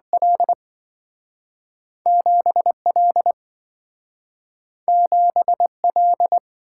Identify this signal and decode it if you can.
Morse code, heard as a keyed tone, about 27 wpm, 700 Hz. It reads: L 7L 7L